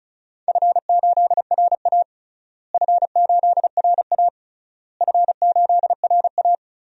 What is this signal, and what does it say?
Morse code, heard as a keyed tone, about 36 wpm, 700 Hz. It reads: F8RA F8RA F8RA
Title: DXCC-100 Call Signs - 4 Characters - Repeated 3x - 35wpm